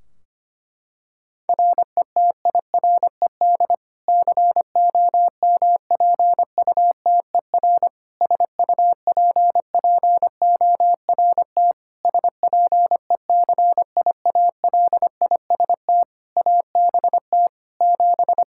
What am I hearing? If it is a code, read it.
Morse code, heard as a keyed tone, about 25 wpm, 700 Hz. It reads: RETIRED COMPUTER SUPPORT SPECIALIST ABT 7
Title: Practice QSO - Exchange 3 - More Info - 25wpm